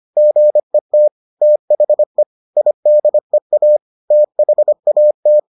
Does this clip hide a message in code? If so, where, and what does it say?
Yes: Morse code from 0.2 to 5.4 s, heard as a keyed tone, about 25 wpm, 600 Hz. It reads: GET THE IDEA THAT